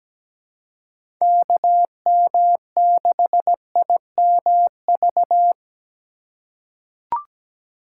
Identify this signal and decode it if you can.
Morse code, heard as a keyed tone, about 17 wpm, 700 Hz. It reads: KM6IMV
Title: USA Call Signs - 6 Characters - 4x Character Spacing - 17wpm